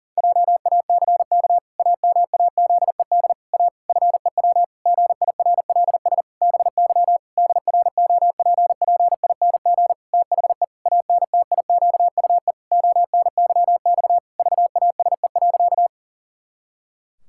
Morse code, heard as a keyed tone, about 40 wpm, 700 Hz. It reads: JACK AMAZED A FEW GIRLS BY DROPPING THE ANTIQUE ONYX VASE.
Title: Pangram Challenge v2 - 40wpm